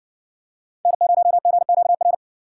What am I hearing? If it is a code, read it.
Morse code, heard as a keyed tone, about 60 wpm, 700 Hz. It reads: N0GQR